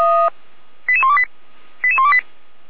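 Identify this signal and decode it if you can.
Five-tone selective call (ZVEI), heard as quick runs of five tones, tones of about 70 ms. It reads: 88128 88128
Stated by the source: ZVEI five-tone selective call received by SDR (455).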